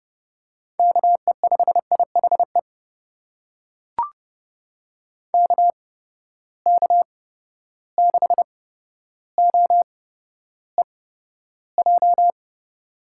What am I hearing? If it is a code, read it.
Morse code, heard as a keyed tone, about 29 wpm, spaced as for 13 wpm, 700 Hz. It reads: KE5IHE KK6OEJ
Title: USA Call Signs - 6 Characters - 8x Character Spacing - 30wpm